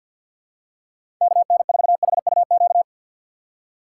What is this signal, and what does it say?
Morse code, heard as a keyed tone, about 52 wpm, 700 Hz. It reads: KN4HUQ
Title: USA Call Signs - 6 Characters - 3x Character Spacing - 50wpm